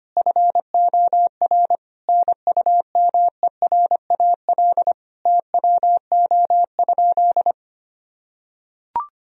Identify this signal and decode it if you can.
Morse code, heard as a keyed tone, about 25 wpm, 700 Hz. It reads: FOR NUMERAL TWO?
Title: Sentences from Top 1000 Words - Review Part 2 - 25wpm